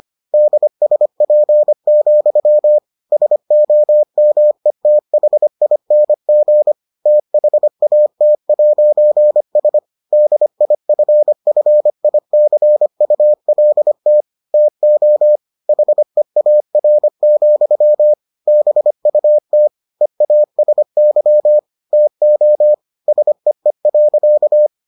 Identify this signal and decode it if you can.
Morse code, heard as a keyed tone, about 25 wpm, 600 Hz. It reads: DSP, SOMETHING THAT'S DIFFICULT TO HEAR, BUT EASY TO SEE.